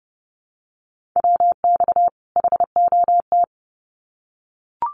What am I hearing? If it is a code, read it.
Morse code, heard as a keyed tone, about 30 wpm, 700 Hz. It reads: WX HOT
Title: Common QSO Phrases 30wpm